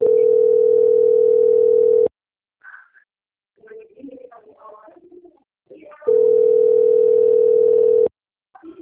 Ringback tone (North America), 440 + 480 Hz.
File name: out- - - - - .32979